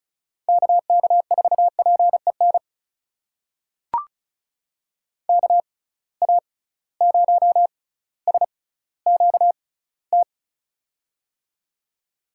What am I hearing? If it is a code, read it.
Morse code, heard as a keyed tone, about 36 wpm, spaced as for 18 wpm, 700 Hz. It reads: KK4PEN KA0SQT
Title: USA Call Signs - 6 Characters - 6x Character Spacing - 35wpm